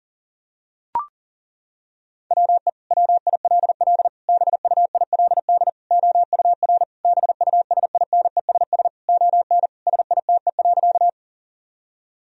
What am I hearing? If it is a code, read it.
Morse code, heard as a keyed tone, about 40 wpm, 700 Hz. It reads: WE WILL BUILD OUR BUSINESS ON SITE.
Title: Sentences from Top 1000 Words - Review Part 4 - 40wpm